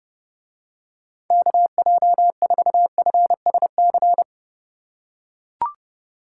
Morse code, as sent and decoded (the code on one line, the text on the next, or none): -.- .--- ....- ..-. ... -.-.
KJ4FSC